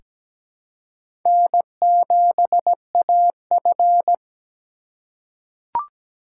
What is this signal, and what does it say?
Morse code, heard as a keyed tone, about 17 wpm, 700 Hz. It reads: N7AF
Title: USA Call Signs - 4 Characters - 2x Character Spacing - 17wpm